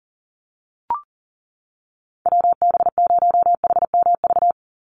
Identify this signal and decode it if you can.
Morse code, heard as a keyed tone, about 40 wpm, 700 Hz. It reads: WB0HMV
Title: USA Call Signs 40wpm